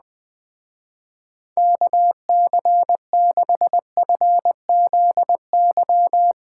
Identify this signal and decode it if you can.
Morse code, heard as a keyed tone, about 20 wpm, 700 Hz. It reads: KC6FZY
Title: Speed Racing - USA Call Signs 17wpm